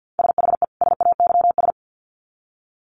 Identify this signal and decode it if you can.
Morse code, heard as a keyed tone, about 50 wpm, 700 Hz. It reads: SHE SAYS
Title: Sets of 2 Words 50wpm